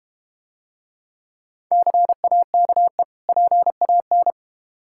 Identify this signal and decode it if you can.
Morse code, heard as a keyed tone, about 30 wpm, 700 Hz. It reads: CAKE PAN
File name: Easy Words - Speed-Racing with Hints - 25wpm